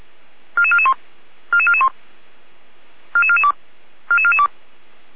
Five-tone selective call (ZVEI), heard as quick runs of five tones, tones of about 70 ms. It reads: 44441 44441 44442 44442